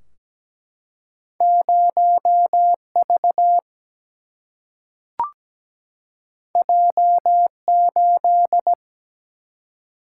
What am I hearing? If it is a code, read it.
Morse code, heard as a keyed tone, about 17 wpm, 700 Hz. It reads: 0V J8
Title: Single-Letter Single-Number 17wpm